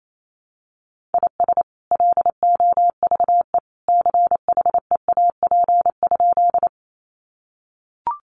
Morse code, heard as a keyed tone, about 28 wpm, 700 Hz. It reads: IS LOVE CHEAP?
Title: Sentences from Top 1200 Words 28wpm